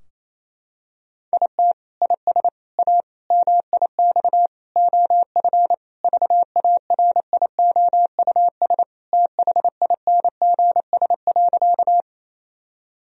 Morse code, heard as a keyed tone, about 28 wpm, 700 Hz. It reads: IT IS A MIX OF VARIOUS THINGS.
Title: Sentences from Top 1000 Words - Review Part 4 - 28wpm